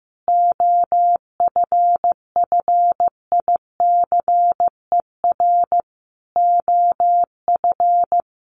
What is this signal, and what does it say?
Morse code, heard as a keyed tone, about 15 wpm, 700 Hz. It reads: OFFICER OF
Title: Sets of 5 Words Encore2 15wpm